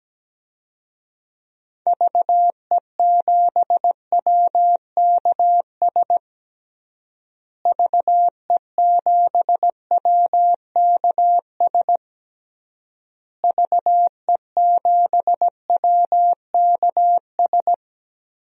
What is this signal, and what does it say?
Morse code, heard as a keyed tone, about 17 wpm, 700 Hz. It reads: VE7WKS VE7WKS VE7WKS
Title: DXCC-100 Call Signs - 6 Characters - Repeated 3x - 17wpm